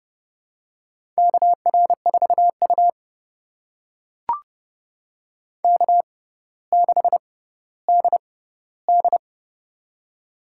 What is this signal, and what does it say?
Morse code, heard as a keyed tone, about 31 wpm, spaced as for 16 wpm, 700 Hz. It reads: KR4U K6DD
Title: USA Call Signs - 4 Characters - 6x Character Spacing - 30wpm